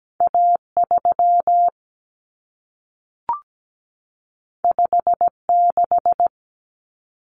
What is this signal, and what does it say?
Morse code, heard as a keyed tone, about 17 wpm, 700 Hz. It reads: A 3 5 6
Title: Common Cut Numbers - 2-Digit 17wpm